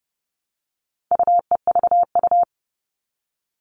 Morse code, heard as a keyed tone, about 30 wpm, 700 Hz. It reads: UEVU
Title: All Cut Numbers - 4-Digit 30wpm